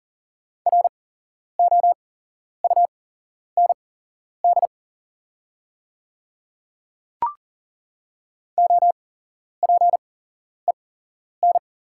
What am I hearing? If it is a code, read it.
Morse code, heard as a keyed tone, about 40 wpm, spaced as for 11 wpm, 700 Hz. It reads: ROUND OPEN
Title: 8x Character Spacing - Top 300 Words - 40wpm